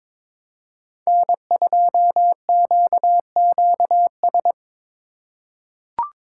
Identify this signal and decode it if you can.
Morse code, heard as a keyed tone, about 22 wpm, 700 Hz. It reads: N2QQS
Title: USA Call Signs - 3x Character Spacing - 22wpm